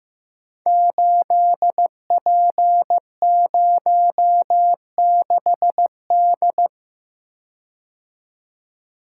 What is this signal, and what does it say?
Morse code, heard as a keyed tone, about 15 wpm, 700 Hz. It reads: 8P06D
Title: Code Groups - 5 Letter:Numbers - 15wpm